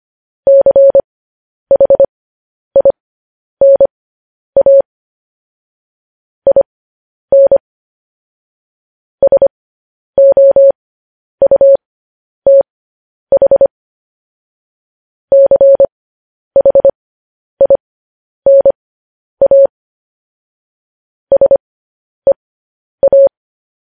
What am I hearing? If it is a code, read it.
Morse code, heard as a keyed tone, about 25 wpm, spaced as for 10 wpm, 550 Hz. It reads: CHINA IN SOUTH CHINA SEA